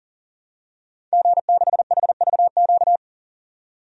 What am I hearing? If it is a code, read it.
Morse code, heard as a keyed tone, about 37 wpm, 700 Hz. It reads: G6HVQ
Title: DXCC-100 Call Signs - 5 Characters - 40wpm